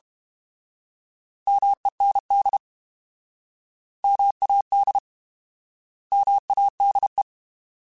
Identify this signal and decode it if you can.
Morse code, heard as a keyed tone, about 32 wpm, 800 Hz. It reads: MEND MAD MADE
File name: Alphabetical / Alphabetical mp3s / Ms @30